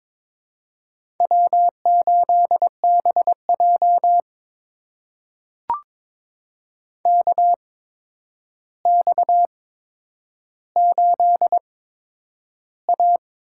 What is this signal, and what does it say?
Morse code, heard as a keyed tone, about 22 wpm, spaced as for 9 wpm, 700 Hz. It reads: W8BJ KX8A
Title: USA Call Signs - 4 Characters - 8x Character Spacing - 22wpm